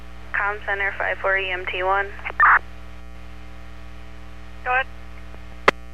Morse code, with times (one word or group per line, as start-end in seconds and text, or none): none